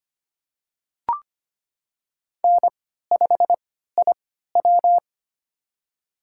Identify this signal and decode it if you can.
Morse code, heard as a keyed tone, about 25 wpm, spaced as for 14 wpm, 700 Hz. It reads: N5IW
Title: USA Call Signs - 3x Character Spacing - 25wpm